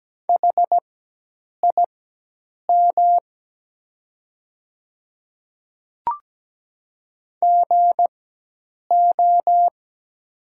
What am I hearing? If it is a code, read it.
Morse code, heard as a keyed tone, about 17 wpm, spaced as for 8 wpm, 700 Hz. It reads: HIM GO
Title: 4x Character Spacing - Top 100 Words - 17wpm